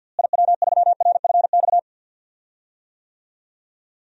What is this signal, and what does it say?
Morse code, heard as a keyed tone, about 52 wpm, 700 Hz. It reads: IW3RFX
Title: DXCC-100 Call Signs - 6 Characters - Repeated 2x - 50wpm